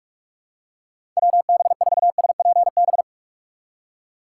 Morse code, heard as a keyed tone, about 46 wpm, 700 Hz. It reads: WB4SPB
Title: K1USN SST QSO Practice - 45wpm